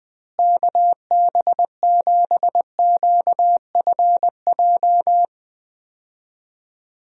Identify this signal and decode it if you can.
Morse code, heard as a keyed tone, about 20 wpm, 700 Hz. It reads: KB7QFJ
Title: USA Call Signs - 6 Characters - Repeated 2x - 20wpm